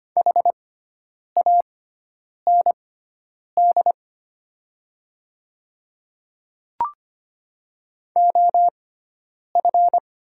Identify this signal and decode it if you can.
Morse code, heard as a keyed tone, about 25 wpm, spaced as for 9 wpm, 690 Hz. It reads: HAND OF
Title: 6x Character Spacing - Top 200 Words - 25wpm